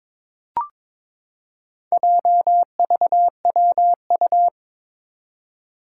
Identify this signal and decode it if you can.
Morse code, heard as a keyed tone, about 22 wpm, 700 Hz. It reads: JVWU
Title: Four Challenging Letter-Number Combinations 22wpm